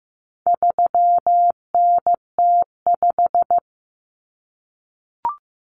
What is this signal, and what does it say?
Morse code, heard as a keyed tone, about 15 wpm, 700 Hz. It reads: 3NT5
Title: Common Cut Numbers - 4-Digit 15wpm